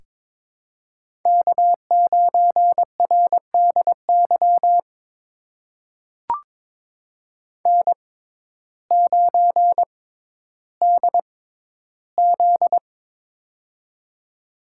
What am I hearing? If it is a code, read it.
Morse code, heard as a keyed tone, about 22 wpm, spaced as for 12 wpm, 700 Hz. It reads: K9RDY N9DZ
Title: USA Call Signs - 6x Character Spacing - 22wpm